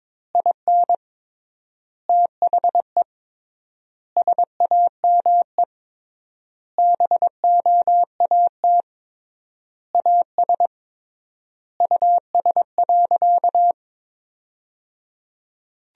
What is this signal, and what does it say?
Morse code, heard as a keyed tone, about 22 wpm, 700 Hz. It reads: IN THE SAME BOAT AS US.
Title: 3x Word-Spacing - Sentences from Top 500 Words - 22wpm